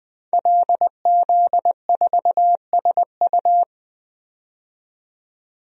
Morse code, heard as a keyed tone, about 20 wpm, 700 Hz. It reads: LZ4SU
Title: DXCC-100 Call Signs - 5 Characters - 20wpm